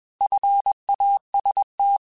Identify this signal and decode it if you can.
Morse code, heard as a keyed tone, about 21 wpm, 800 Hz. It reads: FAST
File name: Alphabetical / Alphabetical mp3s / Fs @20